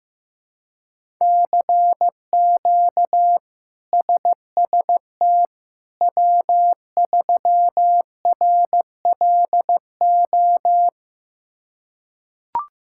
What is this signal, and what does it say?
Morse code, heard as a keyed tone, about 16 wpm, 700 Hz. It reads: CQ SST W3RLO
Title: K1USN SST QSO Practice - 15wpm